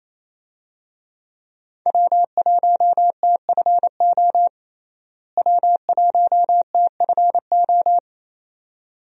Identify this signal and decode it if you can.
Morse code, heard as a keyed tone, about 28 wpm, 700 Hz. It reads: W1TFO W1TFO
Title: USA Call Signs - 5 Characters - Repeated 2x - 28wpm